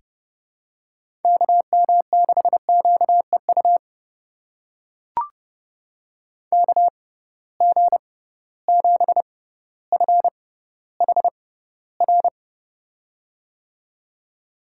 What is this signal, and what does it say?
Morse code, heard as a keyed tone, about 31 wpm, spaced as for 15 wpm, 700 Hz. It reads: KM6QEU KG7FHR
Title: USA Call Signs - 6 Characters - 6x Character Spacing - 30wpm